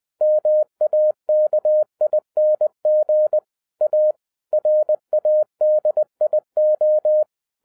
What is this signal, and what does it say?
Morse code, heard as a keyed tone, about 20 wpm, 610 Hz. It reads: MAKING A RADIO
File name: 2012-01-mp3 / n05-CAME BACK TO HAM RADIO AFTER 30 YEARS OF INTERVAL-20wpm